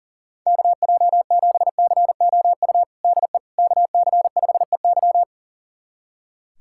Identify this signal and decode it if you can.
Morse code, heard as a keyed tone, about 39 wpm, 700 Hz. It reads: KJ7COU DE KC5EY
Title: USA Callsign DE Callsign 40wpm